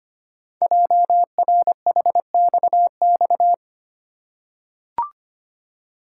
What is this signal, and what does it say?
Morse code, heard as a keyed tone, about 25 wpm, 700 Hz. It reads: JRHXX
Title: Code Groups - 5 Letters - 25wpm